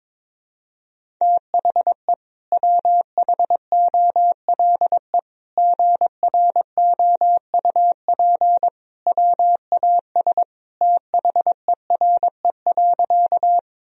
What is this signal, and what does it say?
Morse code, heard as a keyed tone, about 21 wpm, 700 Hz. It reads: THE WHOLE GROUP WAS THERE.
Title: Sentences from Top 1000 Words - Review Part 1 - 22wpm